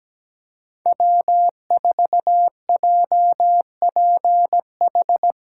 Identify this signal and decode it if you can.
Morse code, heard as a keyed tone, about 17 wpm, 700 Hz. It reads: W4JPH
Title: K1USN SST QSO Practice - 17wpm